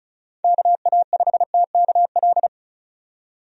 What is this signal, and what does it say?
Morse code, heard as a keyed tone, about 36 wpm, 700 Hz. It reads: KA5TKL
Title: USA Call Signs - 6 Characters - 35wpm